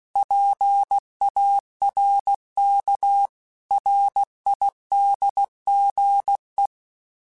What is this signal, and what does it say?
Morse code, heard as a keyed tone, about 16 wpm, 800 Hz. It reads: PARK RIDGE
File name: fromSMs / FromSMs mp3s / Cities@15